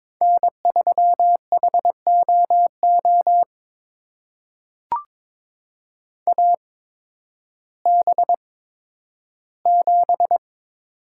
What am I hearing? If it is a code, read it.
Morse code, heard as a keyed tone, about 22 wpm, spaced as for 12 wpm, 700 Hz. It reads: N3HOO AB7
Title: USA Call Signs - 5 Characters - 8x Character Spacing - 22wpm